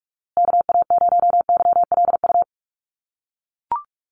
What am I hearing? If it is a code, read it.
Morse code, heard as a keyed tone, about 44 wpm, 700 Hz. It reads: KA0YLU
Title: USA Call Signs Repeated 2x 45wpm